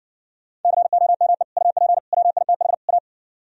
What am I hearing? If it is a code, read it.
Morse code, heard as a keyed tone, about 60 wpm, 700 Hz. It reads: COME UP WITH A